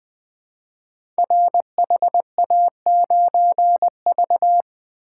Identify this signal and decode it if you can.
Morse code, heard as a keyed tone, about 20 wpm, 700 Hz. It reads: RHA9V
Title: Code Groups - 5 Letter:Numbers - 20wpm